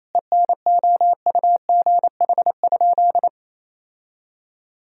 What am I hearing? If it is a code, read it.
Morse code, heard as a keyed tone, about 28 wpm, 700 Hz. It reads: ENOUGH?
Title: Sentences from Top 1000 Words - Review Part 2 - 28wpm